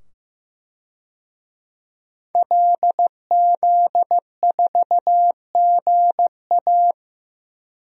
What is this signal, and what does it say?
Morse code, heard as a keyed tone, about 15 wpm, 700 Hz. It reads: LZ4GA
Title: DXCC-100 Call Signs - 5 Characters - 15wpm